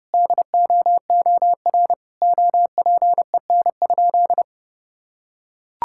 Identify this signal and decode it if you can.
Morse code, heard as a keyed tone, about 32 wpm, 700 Hz. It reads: DOOR OPEN?
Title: Sentences from Top 1600 Words 30wpm